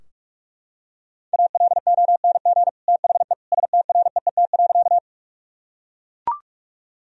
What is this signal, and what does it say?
Morse code, heard as a keyed tone, about 44 wpm, 700 Hz. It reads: ALONG THE STREET.
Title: Sentences from Top 1000 Words - Review Part 3 - 45wpm